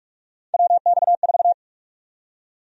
Morse code, heard as a keyed tone, about 44 wpm, 700 Hz. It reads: WX4
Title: USA Call Sign Prefixes - Repeated 2x - 45wpm